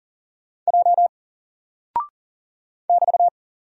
Morse code, heard as a keyed tone, about 40 wpm, 700 Hz. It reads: J <BT>
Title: Single Character (Letter:Number:Punctuation:Prosign) Alt 40wpm